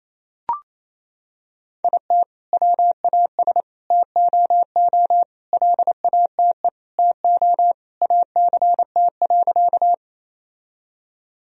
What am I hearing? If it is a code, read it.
Morse code, heard as a keyed tone, about 28 wpm, 700 Hz. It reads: IT WAS TOO LATE TO ACT.
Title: Sentences from Top 1000 Words - Review Part 1 - 28wpm